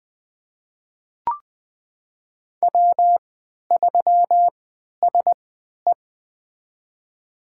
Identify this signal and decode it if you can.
Morse code, heard as a keyed tone, about 20 wpm, spaced as for 11 wpm, 700 Hz. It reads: W3SE
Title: USA Call Signs - 4 Characters - 3x Character Spacing - 20wpm